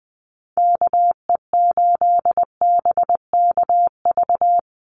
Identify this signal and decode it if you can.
Morse code, heard as a keyed tone, about 20 wpm, 700 Hz. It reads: KE8BKV
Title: USA Call Signs 20wpm